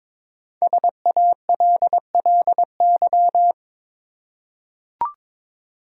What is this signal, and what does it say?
Morse code, heard as a keyed tone, about 22 wpm, 700 Hz. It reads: SALLY
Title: USA Common Names - Max 5 Letters - 22wpm